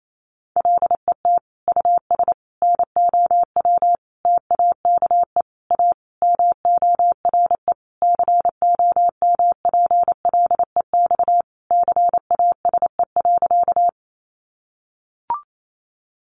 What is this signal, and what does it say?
Morse code, heard as a keyed tone, about 28 wpm, 700 Hz. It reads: LET US NOW TAKE A MORE COMPLEX CASE.
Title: Sentences from Top 900 Words 28wpm